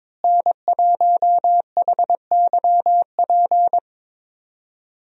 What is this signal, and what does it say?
Morse code, heard as a keyed tone, about 22 wpm, 700 Hz. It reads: N1HYP
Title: USA Call Signs - 5 Characters - Repeated 2x - 22wpm